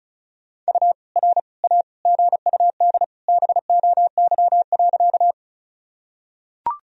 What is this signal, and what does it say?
Morse code, heard as a keyed tone, about 34 wpm, 690 Hz. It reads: U R A GUD BOY.
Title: Sentences from Top 200 Words and Abbreviations 35wpm